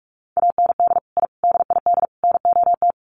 Morse code, heard as a keyed tone, about 44 wpm, 700 Hz. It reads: AND I DID NOT
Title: Sets of 4 Words 45wpm